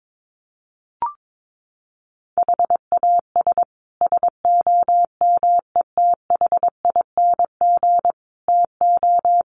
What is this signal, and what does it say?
Morse code, heard as a keyed tone, about 22 wpm, 700 Hz. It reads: HAS SOMETHING TO
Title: Sets of 5 Words Encore 22wpm